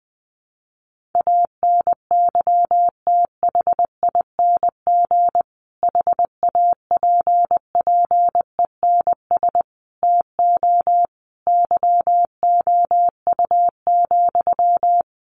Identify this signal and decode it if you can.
Morse code, heard as a keyed tone, about 20 wpm, 700 Hz. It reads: ANYTHING HAPPENS TO YOU,
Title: A Sound Of Thunder 20wpm